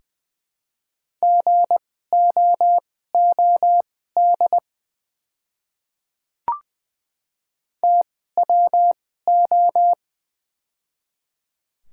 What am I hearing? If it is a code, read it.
Morse code, heard as a keyed tone, about 20 wpm, spaced as for 14 wpm, 690 Hz. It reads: GOOD TWO
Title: 2x Character Spacing - Top 500 Words - 20wpm